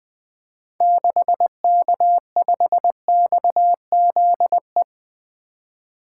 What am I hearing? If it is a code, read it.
Morse code, heard as a keyed tone, about 21 wpm, 700 Hz. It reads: 6K5XZE
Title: 2023 SOTA Activator Call Signs - 20wpm